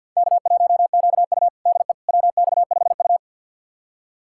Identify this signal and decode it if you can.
Morse code, heard as a keyed tone, about 48 wpm, 700 Hz. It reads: K1QU DE WX5U
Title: USA Callsign DE Callsign - 4 Characters - 50wpm